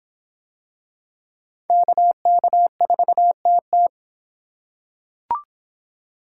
Morse code, heard as a keyed tone, about 26 wpm, 700 Hz. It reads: KK4TT